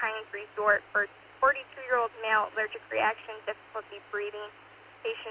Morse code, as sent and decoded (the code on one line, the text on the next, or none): none